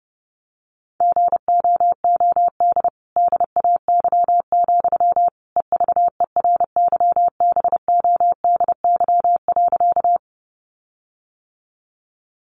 Morse code, heard as a keyed tone, about 30 wpm, 700 Hz. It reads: GOOD DAY, EVERYBODY.
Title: Sentences from Top 1400 Words 30wpm